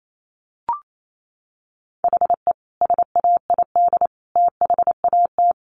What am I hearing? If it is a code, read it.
Morse code, heard as a keyed tone, about 28 wpm, 700 Hz. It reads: HE SAID THAT
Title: Sentences from Top 200 Words 28wpm